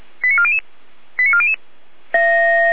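Five-tone selective call (ZVEI), heard as quick runs of five tones, tones of about 80 ms. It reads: 89400 89400
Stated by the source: ZVEI five-tone selective call for a fire brigade, received on an SDR.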